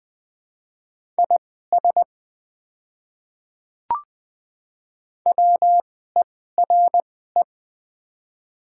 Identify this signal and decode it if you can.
Morse code, heard as a keyed tone, about 20 wpm, spaced as for 14 wpm, 700 Hz. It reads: IS WERE
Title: 2x Character Spacing - Top 100 Words - 20wpm